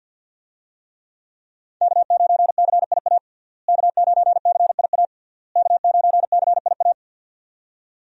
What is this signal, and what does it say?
Morse code, heard as a keyed tone, about 48 wpm, 690 Hz. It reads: K9CIA K9CIA K9CIA
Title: USA Call Signs - 5 Characters - Repeated 3x - 50wpm